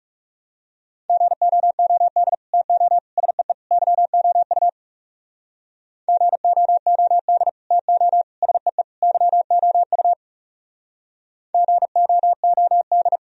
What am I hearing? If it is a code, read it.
Morse code, heard as a keyed tone, about 37 wpm, 700 Hz. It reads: GOOD TO SEE YOU GOOD TO SEE YOU GOOD
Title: Speed-Racing - Sets of 4 Words 30wpm